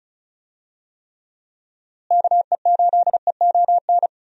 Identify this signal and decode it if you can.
Morse code, heard as a keyed tone, about 36 wpm, 700 Hz. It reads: KE8EON
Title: USA Call Signs - 6 Characters - Repeated 2x - 35wpm